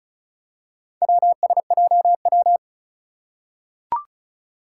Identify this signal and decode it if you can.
Morse code, heard as a keyed tone, about 36 wpm, 700 Hz. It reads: WSJW
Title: Four Challenging Letter-Number Combinations 35wpm